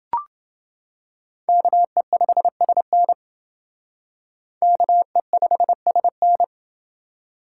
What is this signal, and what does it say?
Morse code, heard as a keyed tone, about 28 wpm, 700 Hz. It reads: KE5SN KE5SN